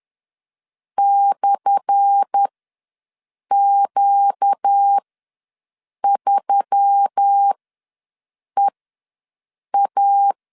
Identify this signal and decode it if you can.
Morse code, heard as a keyed tone, about 10 wpm, spaced as for 6 wpm, 800 Hz. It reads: /Q3EA
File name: LEZIONI / 10 SIGNS / 10A SLASH ws 10wpm 4121